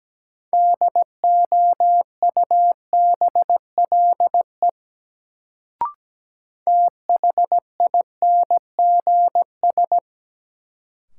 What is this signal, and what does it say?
Morse code, heard as a keyed tone, about 17 wpm, 700 Hz. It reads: DOUBLE THINGS
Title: Sentences from Top 1400 Words - New Word Drill Alt - 17wpm